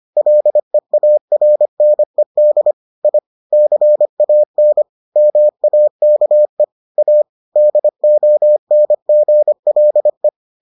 Morse code, heard as a keyed tone, about 25 wpm, 600 Hz. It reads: LEARNED I CAN MAKE A DONGLE